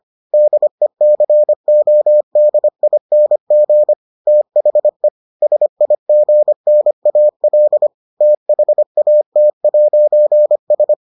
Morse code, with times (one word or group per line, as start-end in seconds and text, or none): 0.3-3.9 DECODING
4.3-5.1 THE
5.4-7.9 SIGNAL
8.2-10.9 THAT'S